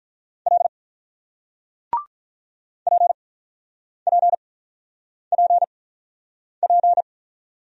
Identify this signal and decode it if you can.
Morse code, heard as a keyed tone, about 48 wpm, 700 Hz. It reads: L P P P P